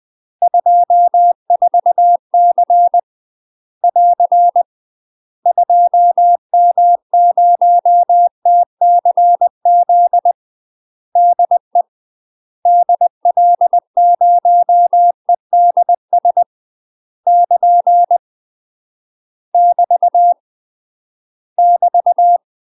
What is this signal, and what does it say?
Morse code, heard as a keyed tone, about 20 wpm, 700 Hz. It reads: 24C <AR> 2M0TCZ DE DL0EDS <KN> <BT> <BT>